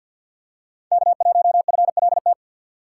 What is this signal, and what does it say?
Morse code, heard as a keyed tone, about 52 wpm, 700 Hz. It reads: K1FLT
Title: USA Call Signs - 5 Characters - 6x Character Spacing - 50wpm